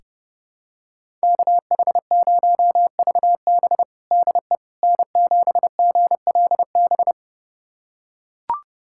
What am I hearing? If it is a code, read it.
Morse code, heard as a keyed tone, about 32 wpm, 700 Hz. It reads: KH0VB DE N7GLB
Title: USA Callsign DE Callsign - 5 Characters - 30wpm